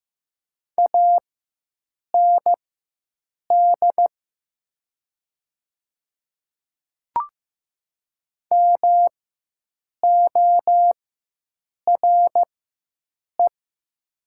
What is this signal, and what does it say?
Morse code, heard as a keyed tone, about 15 wpm, spaced as for 7 wpm, 700 Hz. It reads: AND MORE
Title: 4x Character Spacing - Top 100 Words - 15wpm